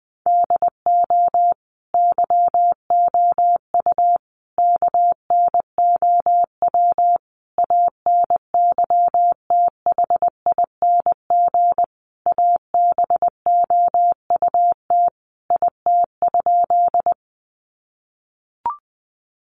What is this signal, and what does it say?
Morse code, heard as a keyed tone, about 20 wpm, 700 Hz. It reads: DO YOU KNOW ANYTHING ABOUT IT?
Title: Sentences from Top 1000 Words 20wpm